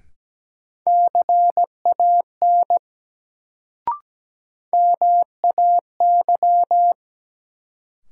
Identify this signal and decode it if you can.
Morse code, heard as a keyed tone, about 17 wpm, 700 Hz. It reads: CAN MAY
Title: Sentences from Top 100 Words - New Word Drill - 17wpm